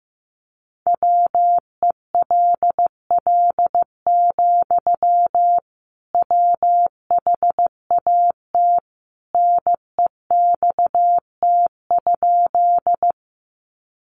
Morse code, heard as a keyed tone, about 15 wpm, 700 Hz. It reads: WELL, WHAT NEXT?
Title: Sentences from Top 300 Words 15wpm